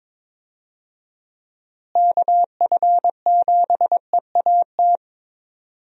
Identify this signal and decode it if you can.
Morse code, heard as a keyed tone, about 22 wpm, 700 Hz. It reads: KF7EAT
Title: USA Call Signs - 6 Characters - Repeated 3x - 22wpm